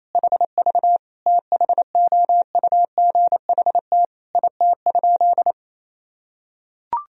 Morse code, heard as a keyed tone, about 28 wpm, 700 Hz. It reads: HV THOUGHT IT?
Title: Sentences from Top 300 Words and Abbreviations 28wpm